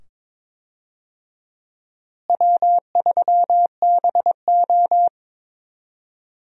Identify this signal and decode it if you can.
Morse code, heard as a keyed tone, about 22 wpm, 700 Hz. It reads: W3BO
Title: USA Call Signs - 4 Characters - Repeated 2x - 22wpm